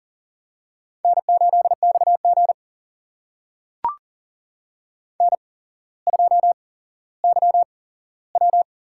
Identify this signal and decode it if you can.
Morse code, heard as a keyed tone, about 38 wpm, spaced as for 17 wpm, 700 Hz. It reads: N8XG N2YW
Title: USA Call Signs - 4 Characters - 8x Character Spacing - 40wpm